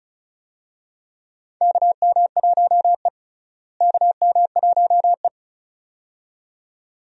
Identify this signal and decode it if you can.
Morse code, heard as a keyed tone, about 36 wpm, 700 Hz. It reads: KM1E KM1E
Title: USA Call Signs - 4 Characters - Repeated 2x - 35wpm